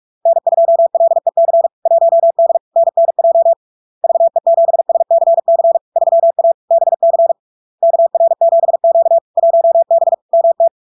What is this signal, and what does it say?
Morse code, heard as a keyed tone, about 46 wpm, 670 Hz. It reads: N2LEC 1D NNJ VE7SCC 3A BC KR7Q 1B MT
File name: LARK-FD-2022-Log-in-CW-45wpm-670-Hz